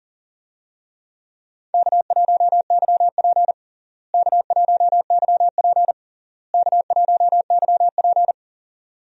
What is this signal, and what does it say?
Morse code, heard as a keyed tone, about 40 wpm, 700 Hz. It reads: K1YP K1YP K1YP
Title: USA Call Signs - 4 Characters - Repeated 3x - 40wpm